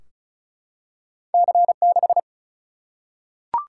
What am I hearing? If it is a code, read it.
Morse code, heard as a keyed tone, about 36 wpm, 700 Hz. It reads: C6
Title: Single-Letter Single-Number 35wpm